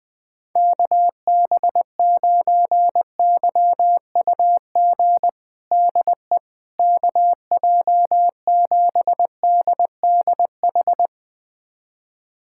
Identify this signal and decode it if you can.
Morse code, heard as a keyed tone, about 20 wpm, 700 Hz. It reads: KB9YUG DE KJ7DDH
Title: USA Callsign DE Callsign - 6 Characters - 20wpm